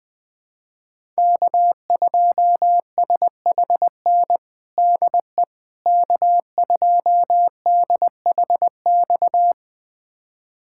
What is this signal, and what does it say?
Morse code, heard as a keyed tone, about 20 wpm, 700 Hz. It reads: K2SHN DE K2DHX
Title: USA Callsign DE Callsign - 5 Characters - 20wpm